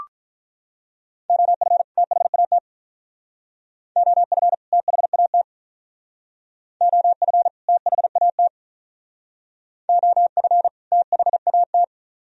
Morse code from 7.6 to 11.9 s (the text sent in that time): THAT OF THAT